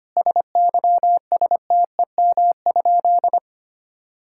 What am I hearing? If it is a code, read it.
Morse code, heard as a keyed tone, about 25 wpm, 700 Hz. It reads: SYSTEM?
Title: Sentences from Top 1000 Words - Review Part 6 - 25wpm